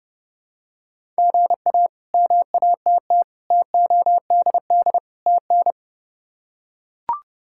Morse code, heard as a keyed tone, about 30 wpm, 700 Hz. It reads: GA MATT TODD TN
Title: K1USN SST QSO Practice - 30wpm